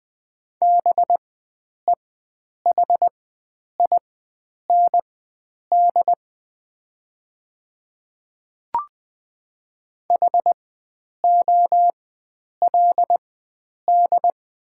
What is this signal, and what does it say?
Morse code, heard as a keyed tone, about 20 wpm, spaced as for 9 wpm, 700 Hz. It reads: BEHIND HOLD
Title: 4x Character Spacing - Top 500 Words - 20wpm